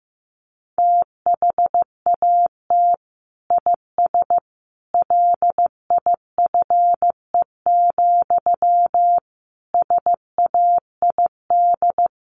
Morse code, heard as a keyed tone, about 15 wpm, 700 Hz. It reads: THAT IS LIFE, SAID
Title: Sentences from Top 900 Words 15wpm